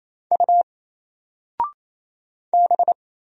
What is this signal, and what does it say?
Morse code, heard as a keyed tone, about 27 wpm, 700 Hz. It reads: U B
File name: Single-Letter-Number-28wpm